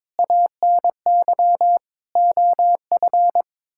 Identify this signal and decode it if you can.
Morse code, heard as a keyed tone, about 22 wpm, 700 Hz. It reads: ANY OF
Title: Sentences from Top 1000 Words - Review Part 6 - 22wpm